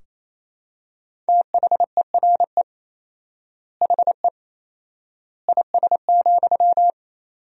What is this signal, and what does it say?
Morse code, heard as a keyed tone, about 28 wpm, 700 Hz. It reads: THERE HE IS,
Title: 4x Word-Spacing - Sentences from Top 100 Words - 28wpm